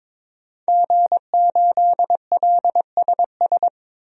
Morse code, heard as a keyed tone, about 22 wpm, 700 Hz. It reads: G8LSS
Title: DXCC-100 Call Signs - 5 Characters - 22wpm